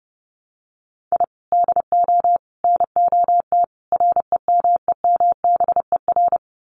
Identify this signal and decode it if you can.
Morse code, heard as a keyed tone, about 30 wpm, 700 Hz. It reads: I DO NOT REMEMBER
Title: Sets of 4 Words 30wpm